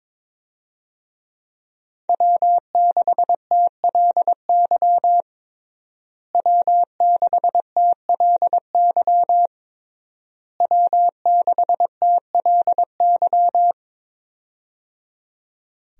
Morse code, heard as a keyed tone, about 22 wpm, 700 Hz. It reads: W6TLY W6TLY W6TLY
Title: USA Call Signs - 5 Characters - Repeated 3x - 22wpm